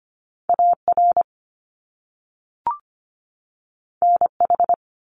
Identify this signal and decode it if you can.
Morse code, heard as a keyed tone, about 25 wpm, 700 Hz. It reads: AR NH
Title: US States 25wpm